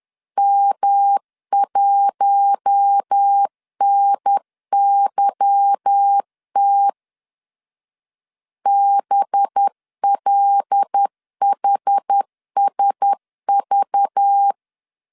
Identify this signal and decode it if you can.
Morse code, heard as a keyed tone, about 10 wpm, 800 Hz. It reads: M1NYT BLHSV
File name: LEZIONI / 9 2 10wpm 2736